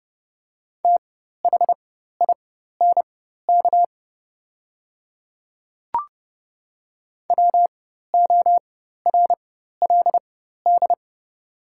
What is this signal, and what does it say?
Morse code, heard as a keyed tone, about 30 wpm, spaced as for 14 wpm, 700 Hz. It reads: THINK WORLD
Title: 4x Character Spacing - Top 200 Words - 30wpm